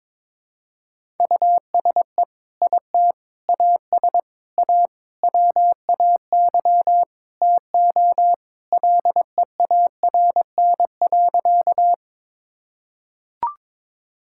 Morse code, heard as a keyed tone, about 22 wpm, 700 Hz. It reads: USE IT AS A WAY TO LEARN.
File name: practice / 22wpm / Sentences with 4-Word Phrases - 22wpm